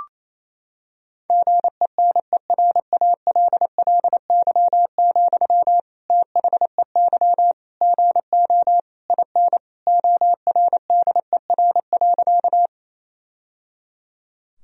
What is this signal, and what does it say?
Morse code, heard as a keyed tone, about 28 wpm, 700 Hz. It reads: GENERALLY, THEY GO IN ORDER.
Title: Sentences from Top 1000 Words - Review Part 4 - 28wpm